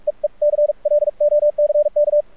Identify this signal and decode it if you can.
Morse code, heard as a keyed tone, about 47 wpm, spaced as for 40 wpm, 590 Hz. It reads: EE CLOCK
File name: archived / CRSnet